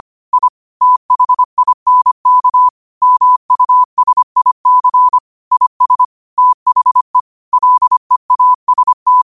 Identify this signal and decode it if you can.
Morse code, heard as a keyed tone, about 25 wpm, 1000 Hz. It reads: I THINK MUSIC IS THE LEAST
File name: qotdfiles / 25wpm